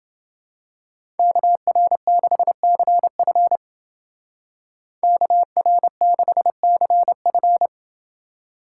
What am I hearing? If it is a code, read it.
Morse code, heard as a keyed tone, about 29 wpm, 700 Hz. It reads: KR6CF KR6CF